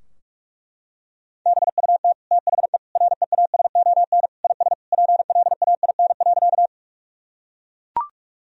Morse code, heard as a keyed tone, about 41 wpm, 700 Hz. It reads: BUT THE REASON IS PLAIN.
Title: Sentences from Top 1000 Words - Review Part 2 - 45wpm